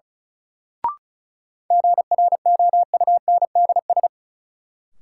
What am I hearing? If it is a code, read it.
Morse code, heard as a keyed tone, about 34 wpm, 700 Hz. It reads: GROUNDS
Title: Sentences from Top 1300 Words - New Word Drill - 35wpm